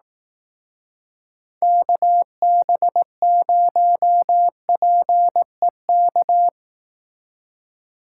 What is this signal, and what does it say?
Morse code, heard as a keyed tone, about 18 wpm, 700 Hz. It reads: KB0PEK
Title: Speed Racing - USA Call Signs 15wpm